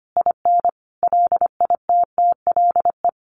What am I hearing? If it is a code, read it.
Morse code, heard as a keyed tone, about 25 wpm, 700 Hz. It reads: IN LITTLE
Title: A Sound Of Thunder 25wpm